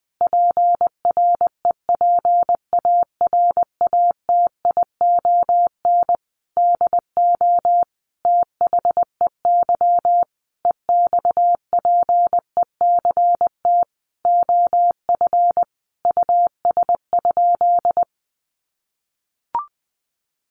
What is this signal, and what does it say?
Morse code, heard as a keyed tone, about 20 wpm, 700 Hz. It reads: PREPARATION DO THEY EXPECT OF US?
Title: Sentences from Top 1400 Words 20wpm